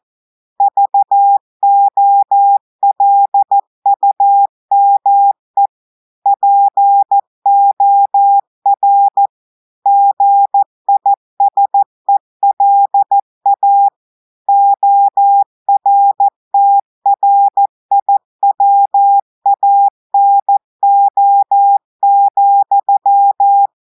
Morse code, heard as a keyed tone, about 14 wpm, 800 Hz. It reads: VOLUME POR GISELA ORTRIWANO,
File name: audios / portugais / portugais texte en clair 1 vitesse 14 mots minutes